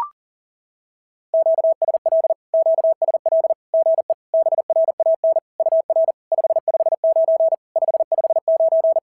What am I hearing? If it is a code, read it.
Morse code, heard as a keyed tone, about 43 wpm, 650 Hz. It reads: QSL QSL GE BRAN UR 559 559
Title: SOTA QSO Practice 40wpm